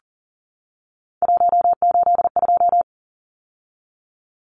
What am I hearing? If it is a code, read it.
Morse code, heard as a keyed tone, about 40 wpm, 700 Hz. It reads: 182
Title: Numbers - 3-Digit 40wpm